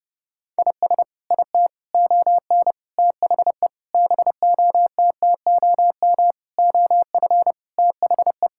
Morse code, heard as a keyed tone, about 32 wpm, 700 Hz. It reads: IS IT ON THE BOTTOM OF THE
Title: Sentences from Top 1000 Words - Review Part 6 - 30wpm